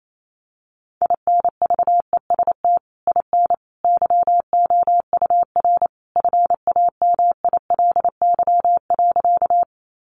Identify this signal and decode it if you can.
Morse code, heard as a keyed tone, about 28 wpm, 700 Hz. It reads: INVEST IN YOUR FAMILY.
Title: Sentences from Top 1100 Words 28wpm